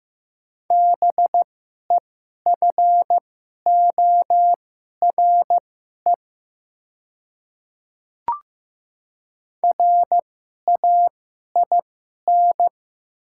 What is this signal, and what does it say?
Morse code, heard as a keyed tone, about 15 wpm, spaced as for 11 wpm, 700 Hz. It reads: BEFORE RAIN
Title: 2x Character Spacing - Top 300 Words - 15wpm